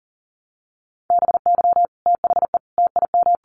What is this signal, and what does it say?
Morse code, heard as a keyed tone, about 40 wpm, 700 Hz. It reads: BY THE TIM
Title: Sets of 5 Words Encore 40wpm